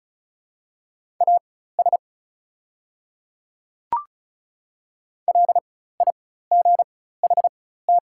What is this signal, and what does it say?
Morse code, heard as a keyed tone, about 35 wpm, spaced as for 16 wpm, 690 Hz. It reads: AS LIGHT
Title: 4x Character Spacing - Top 500 Words - 35wpm